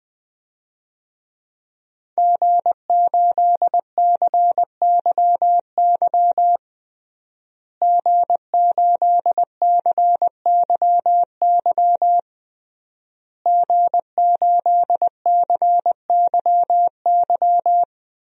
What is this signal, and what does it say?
Morse code, heard as a keyed tone, about 20 wpm, 700 Hz. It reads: G8CYY G8CYY G8CYY
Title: DXCC-100 Call Signs - 5 Characters - Repeated 3x - 20wpm